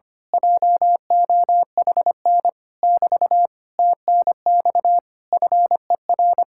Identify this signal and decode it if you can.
Morse code, heard as a keyed tone, about 24 wpm, 700 Hz. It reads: JOHN <BT> TNX FER
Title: Speed-Racing Common QSO Phrases - 25wpm